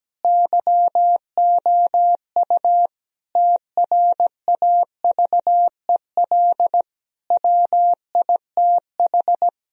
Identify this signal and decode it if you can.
Morse code, heard as a keyed tone, about 17 wpm, 700 Hz. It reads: YOU TRAVEL WITH
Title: Sentences from Top 1000 Words - Review Part 2 - 17wpm